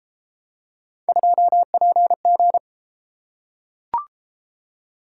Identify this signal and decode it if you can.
Morse code, heard as a keyed tone, about 32 wpm, 700 Hz. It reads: 2PG